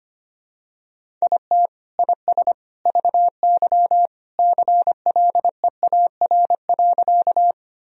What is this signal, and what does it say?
Morse code, heard as a keyed tone, about 25 wpm, 700 Hz. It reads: IT IS VY CLEAR.
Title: Sentences from Top 500 Words and Abbreviations 25wpm